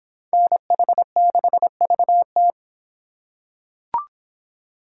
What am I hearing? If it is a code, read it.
Morse code, heard as a keyed tone, about 24 wpm, 700 Hz. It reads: NH6VT